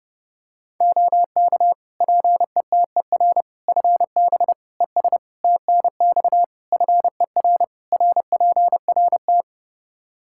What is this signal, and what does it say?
Morse code, heard as a keyed tone, about 30 wpm, 700 Hz. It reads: OK PETER FB ES TNX FER RPRT
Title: K1USN SST QSO Practice 30wpm